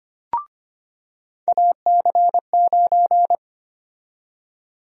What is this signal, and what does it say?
Morse code, heard as a keyed tone, about 25 wpm, 700 Hz. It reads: AC9
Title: USA Call Sign Prefixes - 25wpm